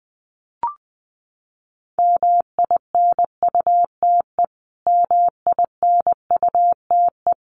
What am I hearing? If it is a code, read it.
Morse code, heard as a keyed tone, about 20 wpm, 700 Hz. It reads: MINUTE MINUTE
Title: Top 500 Words Repeated 20wpm